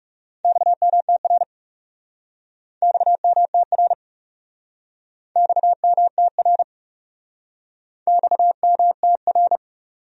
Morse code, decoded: XMTR XMTR XMTR XMTR

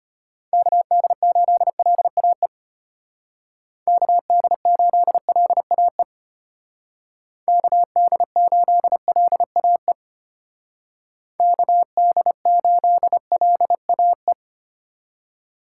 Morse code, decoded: KD8LAE KD8LAE KD8LAE KD8LAE

